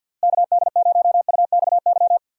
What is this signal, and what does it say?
Morse code, heard as a keyed tone, about 51 wpm, 700 Hz. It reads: KD0UXY
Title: USA Call Signs - 6 Characters - Repeated 3x - 50wpm